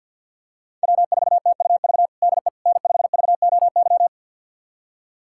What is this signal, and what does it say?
Morse code, heard as a keyed tone, about 49 wpm, 700 Hz. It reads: W4TUV DE N5VOY
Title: USA Callsign DE Callsign - 5 Characters - 50wpm